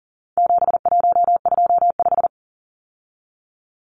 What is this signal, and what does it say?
Morse code, heard as a keyed tone, about 40 wpm, 700 Hz. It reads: 7125
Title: Numbers - 4-Digit 40wpm